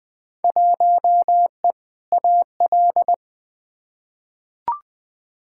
Morse code, heard as a keyed tone, about 20 wpm, 700 Hz. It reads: 1E AL
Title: Field Day Exchanges - 20wpm